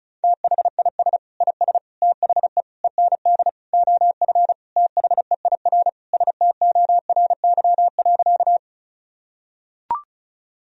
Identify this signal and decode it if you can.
Morse code, heard as a keyed tone, about 34 wpm, 700 Hz. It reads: THIS IS THE END OF THEIR STORY.
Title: Sentences from Top 1000 Words - Review Part 1 - 35wpm